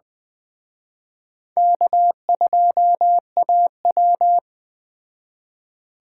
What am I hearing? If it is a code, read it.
Morse code, heard as a keyed tone, about 20 wpm, 700 Hz. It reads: K2AW